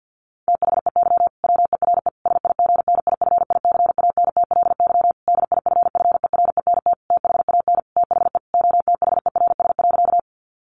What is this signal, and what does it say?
Morse code, heard as a keyed tone, about 50 wpm, 700 Hz. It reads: THEY WERE SIGNIFICANTLY DIFFERENT THAN THE OTHERS.
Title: Sentences from Top 1500 Words 50wpm